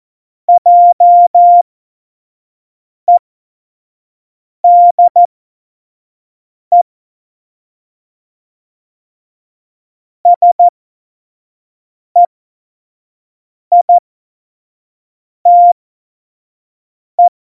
Morse code, heard as a keyed tone, about 14 wpm, spaced as for 5 wpm, 700 Hz. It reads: JEDE SEITE
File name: Morsekurs / Witze / Witz 15. Dezember, F70